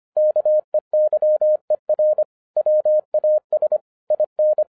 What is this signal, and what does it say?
Morse code, heard as a keyed tone, about 25 wpm, 610 Hz. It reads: KEYER WAS IN
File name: FIRST FISTS CONTACT-25wpm